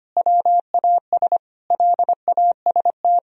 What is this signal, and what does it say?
Morse code, heard as a keyed tone, about 25 wpm, 700 Hz. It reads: WAS LAST
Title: Sentences from Top 1000 Words - Review Part 1 - 25wpm